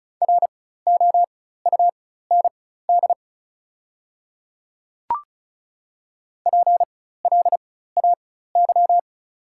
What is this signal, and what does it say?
Morse code, heard as a keyed tone, about 34 wpm, spaced as for 16 wpm, 700 Hz. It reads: ROUND PLAY
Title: 4x Character Spacing - Top 300 Words - 35wpm